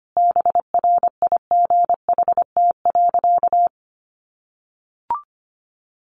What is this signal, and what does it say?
Morse code, heard as a keyed tone, about 25 wpm, 700 Hz. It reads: BRIGHT.
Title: Sentences from Top 1500 Words 25wpm